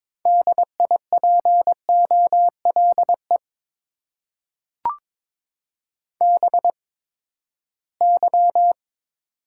Morse code, heard as a keyed tone, about 23 wpm, spaced as for 15 wpm, 700 Hz. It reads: DIPOLE BY
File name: Top 90 QSO Elements - 8x Character Spacing - 22wpm